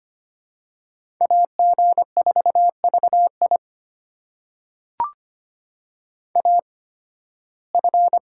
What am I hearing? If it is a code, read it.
Morse code, heard as a keyed tone, about 25 wpm, spaced as for 16 wpm, 700 Hz. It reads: AG4VI AF
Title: USA Call Signs - 5 Characters - 8x Character Spacing - 25wpm